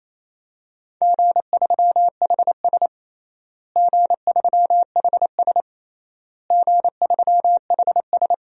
Morse code, heard as a keyed tone, about 28 wpm, 700 Hz. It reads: G3HS G3HS G3HS
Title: DXCC-100 Call Signs - 4 Characters - Repeated 3x - 28wpm